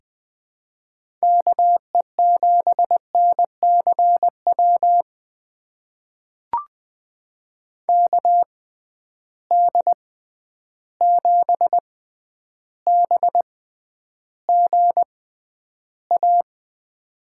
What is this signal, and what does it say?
Morse code, heard as a keyed tone, about 20 wpm, spaced as for 10 wpm, 700 Hz. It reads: KE7NCW KD7BGA
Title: USA Call Signs - 6 Characters - 6x Character Spacing - 20wpm